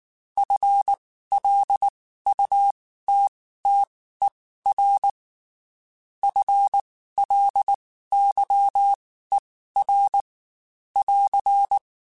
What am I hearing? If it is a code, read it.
Morse code, heard as a keyed tone, about 19 wpm, spaced as for 14 wpm, 800 Hz. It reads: FLUTTER FLYER <AR>
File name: Spaced / FL Runs sp mp3s / FL Runs sp @18